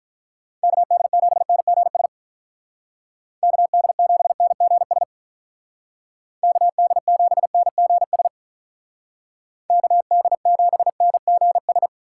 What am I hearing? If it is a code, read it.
Morse code, heard as a keyed tone, about 44 wpm, 700 Hz. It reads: KD7NGS KD7NGS KD7NGS KD7NGS